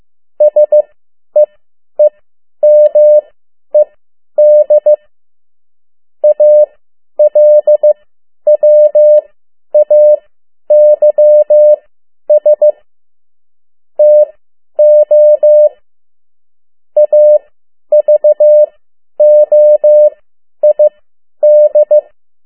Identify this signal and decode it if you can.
Morse code, heard as a keyed tone, about 15 wpm, spaced as for 10 wpm, 600 Hz. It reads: SEEMED ALWAYS TO AVOID
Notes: Morse code 10 WPM